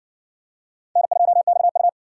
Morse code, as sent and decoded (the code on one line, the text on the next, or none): -. ..--- -..- ..-
N2XU